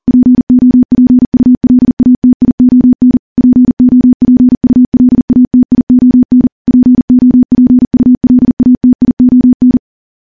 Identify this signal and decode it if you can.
Morse code, heard as a keyed tone, about 40 wpm, 250 Hz. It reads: POPULATION POPULATION POPULATION
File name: morse / morse-training-40WPM